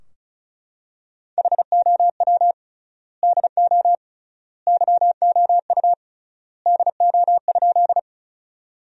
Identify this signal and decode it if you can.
Morse code, heard as a keyed tone, about 36 wpm, 700 Hz. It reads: HOW DO YOU DO?
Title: 3x Word-Spacing - Sentences from Top 100 Words - 35wpm